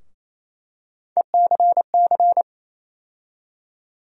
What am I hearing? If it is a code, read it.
Morse code, heard as a keyed tone, about 28 wpm, 700 Hz. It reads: ECC
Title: USA Call Sign Postfixes - 3 Letters 28wpm